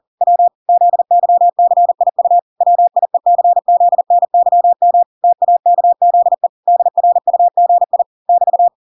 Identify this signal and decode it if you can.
Morse code, heard as a keyed tone, about 40 wpm, 700 Hz. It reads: W ZYCIU WIECZNYM TAKZE DRUGI <BT>